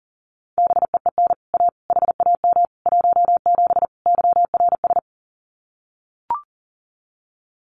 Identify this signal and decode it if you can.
Morse code, heard as a keyed tone, about 40 wpm, 700 Hz. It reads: BEEN A HAM 17 YRS
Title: Common QSO Phrases Encore 40wpm